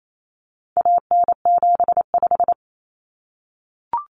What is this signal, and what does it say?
Morse code, heard as a keyed tone, about 28 wpm, 700 Hz. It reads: AN75
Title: Common Cut Numbers - 4-Digit 28wpm